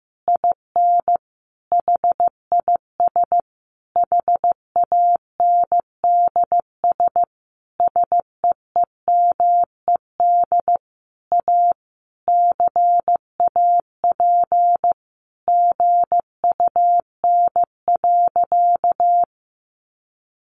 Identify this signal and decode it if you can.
Morse code, heard as a keyed tone, about 15 wpm, 700 Hz. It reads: IN HIS HANDS SEEMED A CAP GUN.
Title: A Sound Of Thunder 15wpm